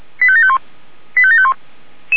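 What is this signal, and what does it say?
Five-tone selective call (ZVEI), heard as quick runs of five tones, tones of about 70 ms. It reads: 85751 85751
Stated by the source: SDR receiver recording of ZVEI five-tone selective call 89389.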